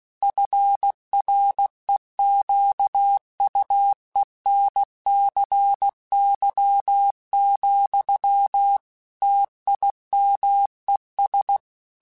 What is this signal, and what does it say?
Morse code, heard as a keyed tone, about 16 wpm, 800 Hz. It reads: FREQUENCY, TIMES
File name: Technical Subjects / Technical Subjects-mp3s / XC @15